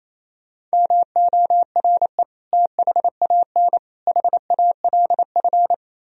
Morse code, heard as a keyed tone, about 37 wpm, 700 Hz. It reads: MORE THAN HALF